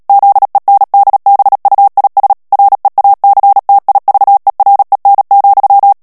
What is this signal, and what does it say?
Morse code, heard as a keyed tone, about 37 wpm, 800 Hz. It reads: ZENDBUIS REACTIVEREN,
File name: morse mp3 / HOLLANDAIS texte clair 2 vitesse 35 mots minute